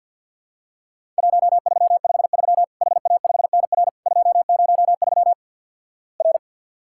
Morse code, heard as a keyed tone, about 48 wpm, 700 Hz. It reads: 1353 HA5NR 203 R
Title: Worked All Europe DX Contest - QTC Exchange with Repeats - 50wpm